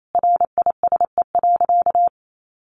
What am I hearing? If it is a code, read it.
Morse code, heard as a keyed tone, about 28 wpm, 700 Hz. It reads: RISE.
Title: Sentences from Top 1300 Words 28wpm